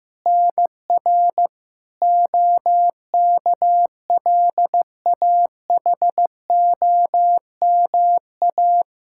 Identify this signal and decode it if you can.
Morse code, heard as a keyed tone, about 15 wpm, 700 Hz. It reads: NR OKLAHOMA
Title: Practice QSO - Exchange 1 - Beginning - 15wpm